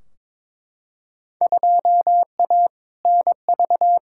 Morse code, heard as a keyed tone, about 22 wpm, 700 Hz. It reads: 2A NV
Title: Field Day Exchanges - 22wpm